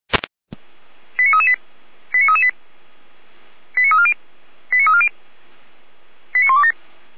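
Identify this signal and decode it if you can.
Five-tone selective call (ZVEI), heard as quick runs of five tones, tones of about 80 ms. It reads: 89338 89338 89340 89340 89127
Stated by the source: ZVEI five-tone selective call received on SDR.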